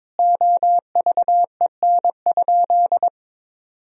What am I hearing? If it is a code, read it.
Morse code, heard as a keyed tone, about 22 wpm, 700 Hz. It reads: OVEN?
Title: Sentences from Top 1000 Words - Review Part 4 - 22wpm